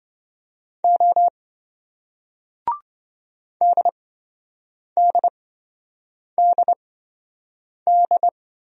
Morse code, heard as a keyed tone, about 26 wpm, spaced as for 7 wpm, 700 Hz. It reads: O DDDD